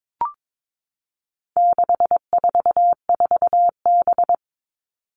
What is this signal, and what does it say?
Morse code, heard as a keyed tone, about 22 wpm, 700 Hz. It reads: 644B
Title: All Cut Numbers - 4-Digit 22wpm